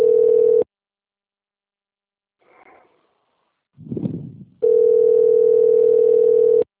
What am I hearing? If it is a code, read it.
Ringback tone (North America), 440 + 480 Hz.